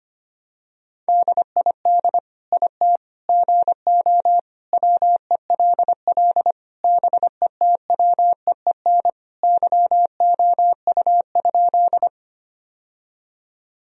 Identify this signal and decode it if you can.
Morse code, heard as a keyed tone, about 25 wpm, 700 Hz. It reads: DID IT GO WELL BETWEEN YOU?
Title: Sentences from Top 1000 Words - Review Part 1 - 25wpm